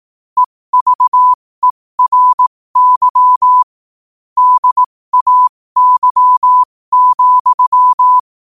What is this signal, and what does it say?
Morse code, heard as a keyed tone, about 18 wpm, spaced as for 15 wpm, 1000 Hz. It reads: EVERY DAY,
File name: qotdfiles / 15wpm